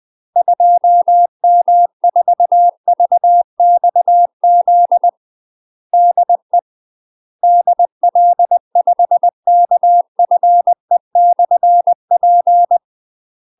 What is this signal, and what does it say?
Morse code, heard as a keyed tone, about 20 wpm, 700 Hz. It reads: 2M4VXZ DE DL5KFE/P
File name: UK MASTER QSOs TRAINING/20WPM